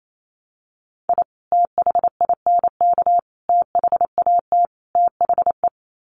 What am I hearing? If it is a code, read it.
Morse code, heard as a keyed tone, about 28 wpm, 700 Hz. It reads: I THINK THAT THE
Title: Sets of 4 Words 28wpm